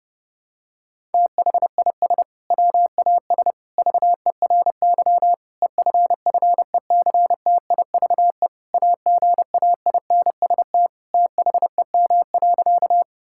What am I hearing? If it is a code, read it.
Morse code, heard as a keyed tone, about 30 wpm, 700 Hz. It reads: THIS WAS VERY EFFECTIVE AGAINST THEM.
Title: Sentences from Top 1000 Words - Review Part 6 - 30wpm